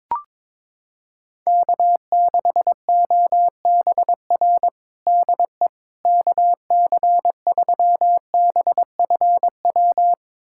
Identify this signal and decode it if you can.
Morse code, heard as a keyed tone, about 22 wpm, 700 Hz. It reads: K6OBR DE KC3BFW
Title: USA Callsign DE Callsign 22wpm